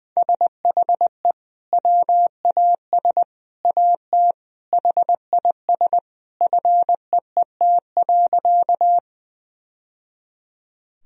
Morse code, heard as a keyed tone, about 20 wpm, 700 Hz. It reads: SHE WAS AT HIS FEET.
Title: Sentences from Top 1000 Words - Review Part 1 - 20wpm